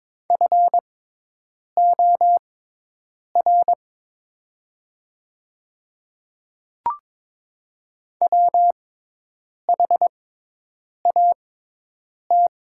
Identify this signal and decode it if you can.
Morse code, heard as a keyed tone, about 22 wpm, spaced as for 8 wpm, 700 Hz. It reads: FOR WHAT
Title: 6x Character Spacing - Top 200 Words - 22wpm